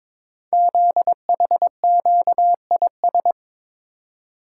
Morse code, heard as a keyed tone, about 23 wpm, 700 Hz. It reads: ZHQIS
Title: Code Groups - 5 Letters - 22wpm